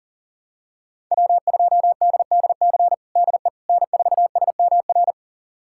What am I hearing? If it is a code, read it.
Morse code, heard as a keyed tone, about 37 wpm, 700 Hz. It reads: W2DDC DE N4SMR
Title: USA Callsign DE Callsign - 5 Characters - 40wpm